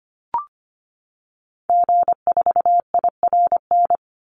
Morse code, heard as a keyed tone, about 25 wpm, 700 Hz. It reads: G4IRN
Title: Call Signs - 25wpm